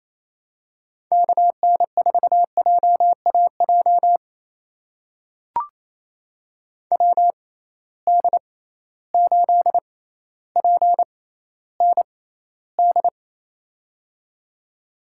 Morse code, heard as a keyed tone, about 28 wpm, spaced as for 14 wpm, 700 Hz. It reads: KN4JAJ WD8PND
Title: USA Call Signs - 6 Characters - 6x Character Spacing - 28wpm